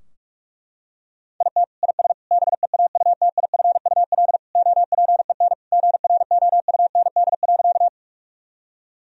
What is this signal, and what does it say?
Morse code, heard as a keyed tone, about 46 wpm, 700 Hz. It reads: IT IS BEAUTIFUL OPEN GROUND.
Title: Sentences from Top 1000 Words - Review Part 2 - 45wpm